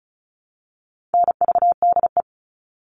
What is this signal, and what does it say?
Morse code, heard as a keyed tone, about 34 wpm, 700 Hz. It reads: NVDE
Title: All Cut Numbers - 4-Digit 35wpm